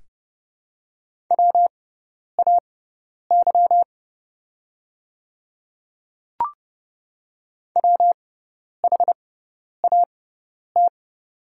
Morse code, heard as a keyed tone, about 31 wpm, spaced as for 10 wpm, 700 Hz. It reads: WAY WHAT
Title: 6x Character Spacing - Top 100 Words - 30wpm